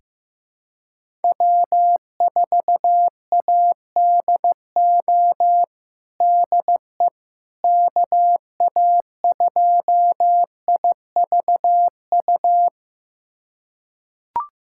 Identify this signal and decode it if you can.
Morse code, heard as a keyed tone, about 15 wpm, 700 Hz. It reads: W4ADO DE KA2IVU
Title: USA Callsign DE Callsign 15wpm